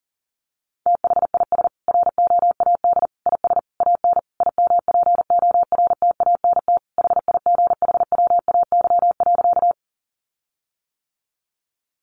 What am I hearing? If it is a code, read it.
Morse code, heard as a keyed tone, about 40 wpm, 700 Hz. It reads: THIS ROAD IS AN IMPORTANT HIGHWAY.
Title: Sentences from Top 1400 Words 40wpm